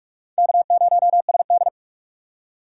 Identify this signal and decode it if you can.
Morse code, heard as a keyed tone, about 45 wpm, 690 Hz. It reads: K0SD
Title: USA Call Signs - 4x Character Spacing - 45wpm